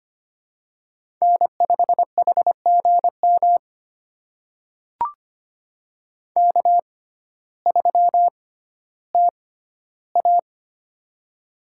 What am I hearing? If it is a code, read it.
Morse code, heard as a keyed tone, about 25 wpm, spaced as for 14 wpm, 700 Hz. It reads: N5HGM K3TA
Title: USA Call Signs - 6x Character Spacing - 25wpm